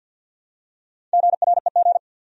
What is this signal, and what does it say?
Morse code, heard as a keyed tone, about 53 wpm, 700 Hz. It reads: GREG
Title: USA Common Names - Max 4 Letters - 50wpm